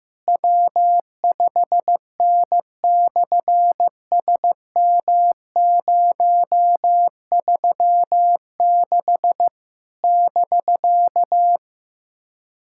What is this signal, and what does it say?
Morse code, heard as a keyed tone, about 14 wpm, 700 Hz. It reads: W5N/SM036 <BK>
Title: SOTA QSO Practice 15wpm